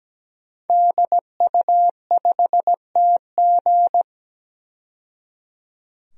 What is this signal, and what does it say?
Morse code, heard as a keyed tone, about 17 wpm, 700 Hz. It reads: DU5TG
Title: DXCC-100 Call Signs - 5 Characters - 17wpm